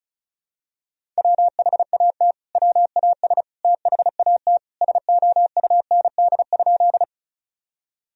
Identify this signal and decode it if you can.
Morse code, heard as a keyed tone, about 36 wpm, 700 Hz. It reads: WHAT WAS THAT SOUND?
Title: Sentences from Top 1000 Words - Review Part 1 - 35wpm